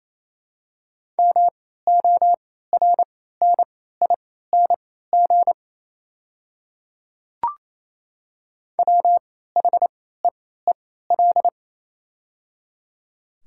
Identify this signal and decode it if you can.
Morse code, heard as a keyed tone, about 28 wpm, spaced as for 16 wpm, 700 Hz. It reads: MORNING WHEEL
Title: 3x Character Spacing - Top 500 Words - 28wpm